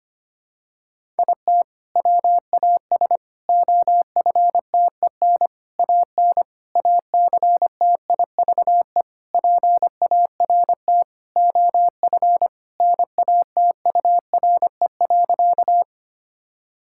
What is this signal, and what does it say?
Morse code, heard as a keyed tone, about 25 wpm, 700 Hz. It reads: IT WAS OFTEN AN ACTIVE PART OF NATURE.
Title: Sentences from Top 1000 Words - Review Part 6 - 25wpm